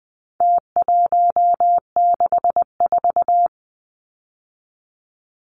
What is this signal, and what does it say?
Morse code, heard as a keyed tone, about 20 wpm, 700 Hz. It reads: T164
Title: Numbers - Contest Style - 4-Digit 20wpm